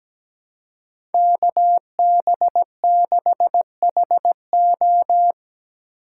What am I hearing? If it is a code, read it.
Morse code, heard as a keyed tone, about 19 wpm, 700 Hz. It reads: KB6HO
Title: Speed Racing - USA Call Signs 17wpm